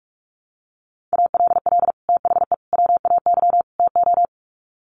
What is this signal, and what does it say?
Morse code, heard as a keyed tone, about 44 wpm, 700 Hz. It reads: ALL THE WAY TO
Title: Sets of 4 Words 45wpm